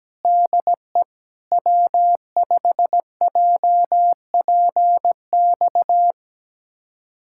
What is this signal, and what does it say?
Morse code, heard as a keyed tone, about 17 wpm, 700 Hz. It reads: DE W5JPX
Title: USA Callsign DE Callsign - 5 Characters - 17wpm